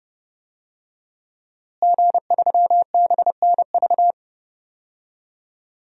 Morse code, heard as a keyed tone, about 30 wpm, 700 Hz. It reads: G3BNV
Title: DXCC-100 Call Signs - 5 Characters - Repeated 3x - 30wpm